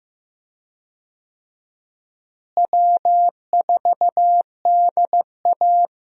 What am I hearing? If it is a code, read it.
Morse code, heard as a keyed tone, about 15 wpm, 700 Hz. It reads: W4DA
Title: USA Call Signs - 4 Characters - Repeated 2x - 15wpm